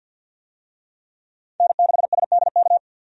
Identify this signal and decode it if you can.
Morse code, heard as a keyed tone, about 51 wpm, 700 Hz. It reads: N6SDK
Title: USA Call Signs - 5 Characters - 50wpm